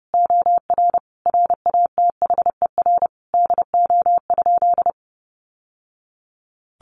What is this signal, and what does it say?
Morse code, heard as a keyed tone, about 30 wpm, 700 Hz. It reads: OR RATHER DO?
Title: Sentences from Top 700 Words 30wpm